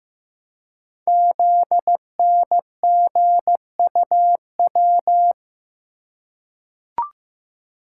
Morse code, heard as a keyed tone, about 15 wpm, 700 Hz. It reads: ZNGUW
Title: Code Groups - 5 Letters - 15wpm